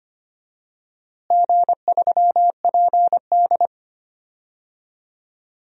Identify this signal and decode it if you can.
Morse code, heard as a keyed tone, about 25 wpm, 700 Hz. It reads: G3PD
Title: DXCC-100 Call Signs - 4 Characters - Repeated 2x - 25wpm